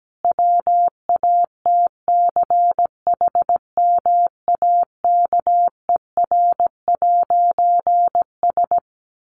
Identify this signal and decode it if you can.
Morse code, heard as a keyed tone, about 17 wpm, 700 Hz. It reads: WATCHMAKER'S
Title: A Sound Of Thunder 17wpm